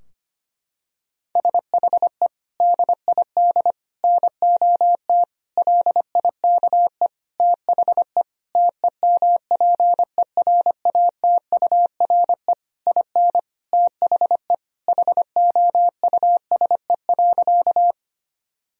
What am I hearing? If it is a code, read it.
Morse code, heard as a keyed tone, about 25 wpm, 700 Hz. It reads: SHE DID NOT LIKE THE TEMPERATURE IN THE HOUSE.
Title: Sentences from Top 1000 Words - Review Part 4 - 25wpm